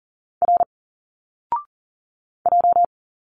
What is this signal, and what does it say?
Morse code, heard as a keyed tone, about 40 wpm, 700 Hz. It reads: R J
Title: Single Letters 40wpm